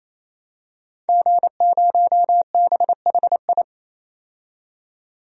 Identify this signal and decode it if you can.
Morse code, heard as a keyed tone, about 28 wpm, 700 Hz. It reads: G0BHI
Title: DXCC-100 Call Signs - 5 Characters - 28wpm